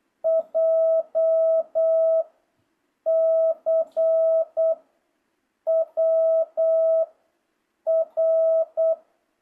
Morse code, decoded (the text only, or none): J C W R